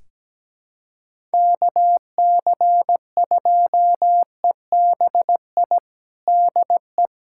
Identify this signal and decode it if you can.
Morse code, heard as a keyed tone, about 17 wpm, 700 Hz. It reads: KC2EBI DE
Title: USA Callsign DE Callsign 17wpm